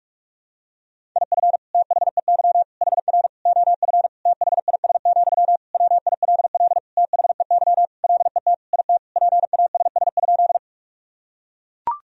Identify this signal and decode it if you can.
Morse code, heard as a keyed tone, about 41 wpm, 700 Hz. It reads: IF THEY HR OF THIS, WILL THEY LET IT PASS?
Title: Sentences from Top 400 Words and Abbreviations 45wpm